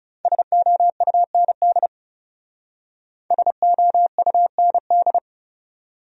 Morse code, heard as a keyed tone, about 37 wpm, 700 Hz. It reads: SOUND SOUND
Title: Speed-Racing Top 500 Words 30wpm